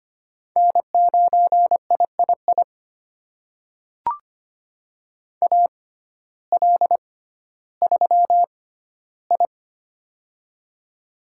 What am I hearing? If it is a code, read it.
Morse code, heard as a keyed tone, about 25 wpm, spaced as for 14 wpm, 700 Hz. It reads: N9III AL3I
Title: USA Call Signs - 6x Character Spacing - 25wpm